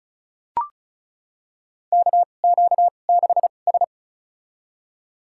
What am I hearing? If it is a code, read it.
Morse code, heard as a keyed tone, about 36 wpm, spaced as for 25 wpm, 690 Hz. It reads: KQ6S
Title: USA Call Signs - 4 Characters - 2x Character Spacing - 35wpm